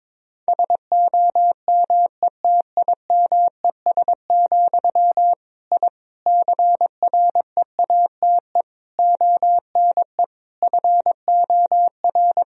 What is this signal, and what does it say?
Morse code, heard as a keyed tone, about 22 wpm, 700 Hz. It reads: SOMETIMES, I CREATE ONE FOR
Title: Sentences from Top 1000 Words - Review Part 3 - 22wpm